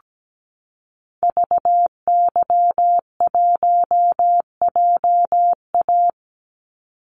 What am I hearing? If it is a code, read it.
Morse code, heard as a keyed tone, about 17 wpm, 700 Hz. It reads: VY1JA
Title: Call Signs Repeated 3x - 17wpm